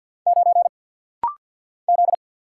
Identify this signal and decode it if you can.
Morse code, heard as a keyed tone, about 50 wpm, 700 Hz. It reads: 9 Z
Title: Single Letter-Number Alt - Rapid-Fire 50wpm